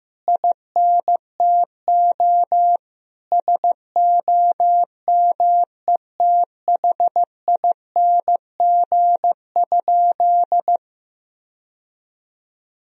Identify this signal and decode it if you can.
Morse code, heard as a keyed tone, about 15 wpm, 700 Hz. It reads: INTO SOMETHING?
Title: Sentences from Top 1000 Words - Review Part 2 - 15wpm